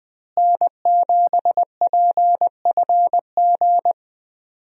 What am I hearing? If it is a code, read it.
Morse code, heard as a keyed tone, about 20 wpm, 700 Hz. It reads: N7PFG
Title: USA Call Signs - 5 Characters - Repeated 2x - 20wpm